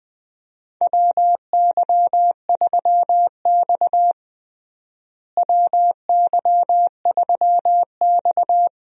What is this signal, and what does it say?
Morse code, heard as a keyed tone, about 21 wpm, 700 Hz. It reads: WY3X WY3X
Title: USA Call Signs - 4 Characters - Repeated 2x - 20wpm